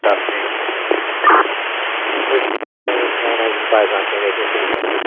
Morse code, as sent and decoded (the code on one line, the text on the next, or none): none